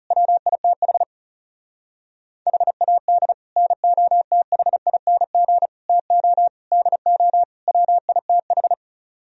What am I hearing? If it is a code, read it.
Morse code, heard as a keyed tone, about 40 wpm, 700 Hz. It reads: WITH HAD NOTHING TO DO WITH